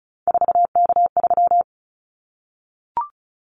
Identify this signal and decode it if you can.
Morse code, heard as a keyed tone, about 34 wpm, 700 Hz. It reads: 4K3
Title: Callsign Prefixes for Top 100 DXCC - 35wpm